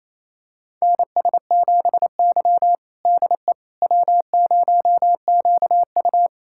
Morse code, heard as a keyed tone, about 27 wpm, 700 Hz. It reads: NS7Y DE W0QU
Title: USA Callsign DE Callsign - 4 Characters - 28wpm